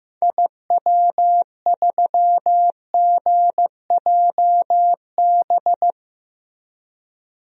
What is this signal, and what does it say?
Morse code, heard as a keyed tone, about 15 wpm, 700 Hz. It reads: IW3GJB
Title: DXCC-100 Call Signs - 6 Characters - Repeated 2x - 15wpm